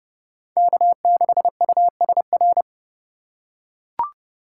Morse code, heard as a keyed tone, about 29 wpm, 700 Hz. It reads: K6USR